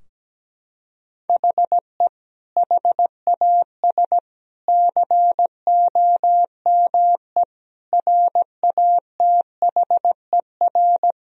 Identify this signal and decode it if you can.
Morse code, heard as a keyed tone, about 17 wpm, 700 Hz. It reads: HE HAS COME RATHER
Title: Sentences from Top 1000 Words - Review Part 4 - 17wpm